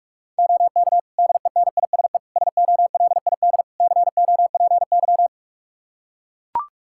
Morse code, heard as a keyed tone, about 43 wpm, 700 Hz. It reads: OK DENISE SOLID COPY
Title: K1USN SST QSO Practice 45wpm